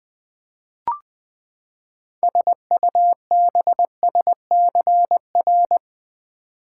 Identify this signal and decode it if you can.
Morse code, heard as a keyed tone, about 20 wpm, 700 Hz. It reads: SUBSCR
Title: Build A Word - 6 Letters At A Time - 12-Letter Words - 20wpm